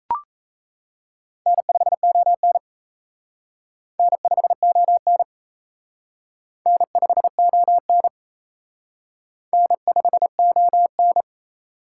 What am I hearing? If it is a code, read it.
Morse code, heard as a keyed tone, about 33 wpm, 700 Hz. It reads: N5ON N5ON N5ON N5ON
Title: Speed Racing - USA Call Signs 28wpm